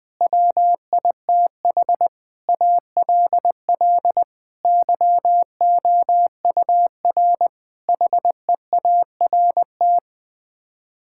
Morse code, heard as a keyed tone, about 20 wpm, 700 Hz. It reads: WITH ALL YOUR HEART
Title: Sets of 9 Words - 20wpm